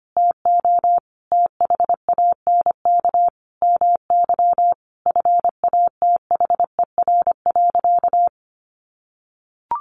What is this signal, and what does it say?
Morse code, heard as a keyed tone, about 25 wpm, 700 Hz. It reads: TO THANK MY FATHER.
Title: Sentences from Top 1100 Words 25wpm